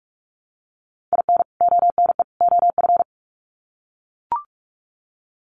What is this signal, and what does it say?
Morse code, heard as a keyed tone, about 44 wpm, 700 Hz. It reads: IN ONE OF
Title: Sets of 3 Words 45wpm